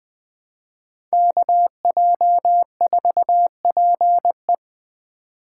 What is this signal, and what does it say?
Morse code, heard as a keyed tone, about 20 wpm, 700 Hz. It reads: KJ4PE
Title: USA Call Signs - 3x Character Spacing - 20wpm